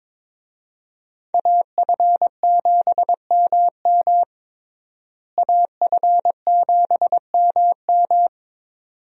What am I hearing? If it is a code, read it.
Morse code, heard as a keyed tone, about 22 wpm, 700 Hz. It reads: AF7MM AF7MM
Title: USA Call Signs - 5 Characters - Repeated 2x - 22wpm